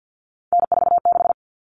Morse code, heard as a keyed tone, about 50 wpm, 700 Hz. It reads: N46
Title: All Cut Numbers - 3-Digit 50wpm